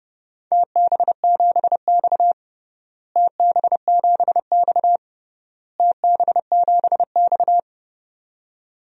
Morse code, heard as a keyed tone, about 31 wpm, 700 Hz. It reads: TB7X TB7X TB7X
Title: DXCC-100 Call Signs - 4 Characters - Repeated 3x - 30wpm